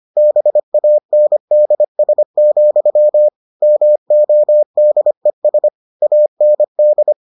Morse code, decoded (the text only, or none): BANDS, MODES AND